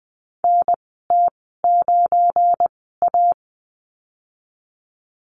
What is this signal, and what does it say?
Morse code, heard as a keyed tone, about 20 wpm, spaced as for 15 wpm, 700 Hz. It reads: NT9A
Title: 173 Lesson 28g - Number 9 - Call Signs 20wpm